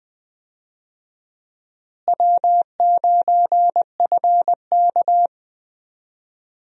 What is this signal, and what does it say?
Morse code, heard as a keyed tone, about 20 wpm, 700 Hz. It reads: W9FK
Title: USA Call Signs - 4 Characters - Repeated 2x - 20wpm